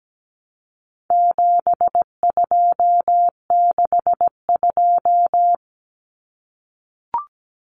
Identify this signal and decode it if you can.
Morse code, heard as a keyed tone, about 17 wpm, 700 Hz. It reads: 7262
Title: Common Cut Numbers - 4-Digit 17wpm